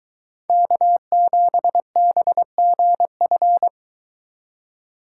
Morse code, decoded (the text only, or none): K7BGF